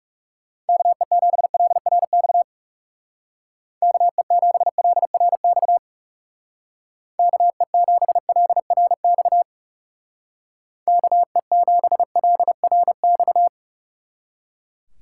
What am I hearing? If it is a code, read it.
Morse code, heard as a keyed tone, about 38 wpm, 700 Hz. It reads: KE7LRX KE7LRX KE7LRX KE7LRX